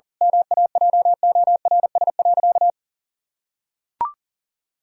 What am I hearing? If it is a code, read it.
Morse code, heard as a keyed tone, about 40 wpm, 700 Hz. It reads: MAJORS.
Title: Pangram Challenge v2 - 40wpm